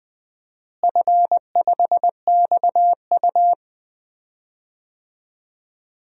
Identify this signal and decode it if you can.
Morse code, heard as a keyed tone, about 20 wpm, 700 Hz. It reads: F5XU
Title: DXCC-100 Call Signs - 4 Characters - Repeated 2x - 20wpm